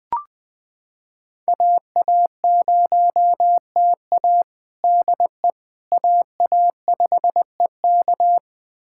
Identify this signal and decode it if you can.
Morse code, heard as a keyed tone, about 20 wpm, 700 Hz. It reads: AA0TA DE AA5EK
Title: USA Callsign DE Callsign - 5 Characters - 20wpm